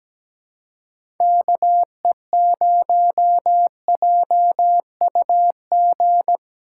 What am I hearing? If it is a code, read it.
Morse code, heard as a keyed tone, about 17 wpm, 700 Hz. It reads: KE0JUG
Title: USA Call Signs - 6 Characters - 3x Character Spacing - 17wpm